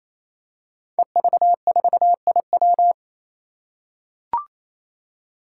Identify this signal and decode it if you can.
Morse code, heard as a keyed tone, about 27 wpm, 700 Hz. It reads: EV4IW
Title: Five Challenging Letter-Number Combinations 28wpm